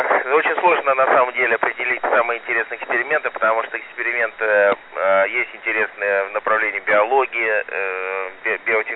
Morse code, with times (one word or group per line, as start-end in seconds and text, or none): none